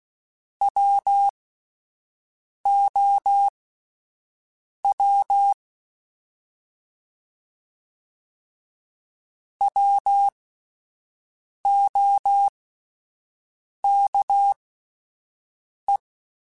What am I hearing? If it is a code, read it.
Morse code, heard as a keyed tone, about 16 wpm, spaced as for 5 wpm, 800 Hz. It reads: WOW WOKE
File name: Alphabetical / Alphabetical mp3s / Ws @5